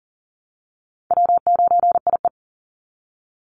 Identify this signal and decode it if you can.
Morse code, heard as a keyed tone, about 40 wpm, 700 Hz. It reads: W9IE
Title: Easy Call Signs Repeated 3x - 40wpm